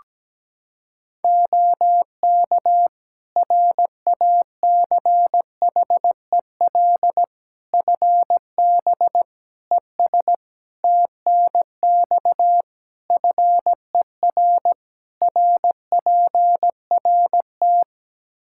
Morse code, heard as a keyed tone, about 17 wpm, 700 Hz. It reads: OK RACHEL FB ES TNX FER RPRT
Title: Practice QSO - Exchange 2 - Info - 17wpm